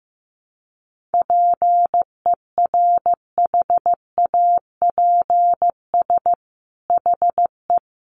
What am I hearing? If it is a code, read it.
Morse code, heard as a keyed tone, about 15 wpm, 700 Hz. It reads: PERHAPS HE
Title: Sentences from Top 1300 Words 15wpm